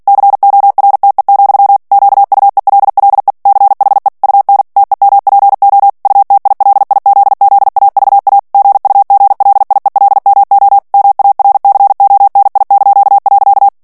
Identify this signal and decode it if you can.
Morse code, heard as a keyed tone, about 48 wpm, 800 Hz. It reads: CORTE, QUELLE CHE UN TEMPO UTILIZZAVA GUGLIELMO MARCONI).
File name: morse mp3 / ITALIEN texte clair 1 vitesse 45 mots minute